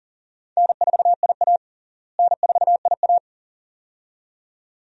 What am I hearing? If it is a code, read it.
Morse code, heard as a keyed tone, about 40 wpm, 700 Hz. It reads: N4IA N4IA
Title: USA Call Signs - 4 Characters - Repeated 2x - 40wpm